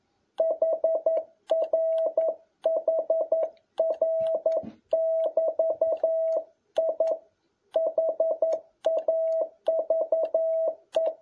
Morse code, heard as a keyed tone, about 11 wpm, 650 Hz. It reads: HRHR<BT>I HAVE
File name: jh8cbh / 2021cw